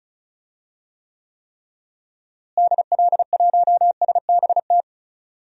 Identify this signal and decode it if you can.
Morse code, heard as a keyed tone, about 36 wpm, 700 Hz. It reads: DL1SBT
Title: DXCC-100 Call Signs - 6 Characters - Repeated 3x - 35wpm